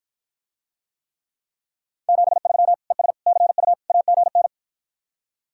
Morse code, heard as a keyed tone, about 54 wpm, 700 Hz. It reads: 73 ES CU AGN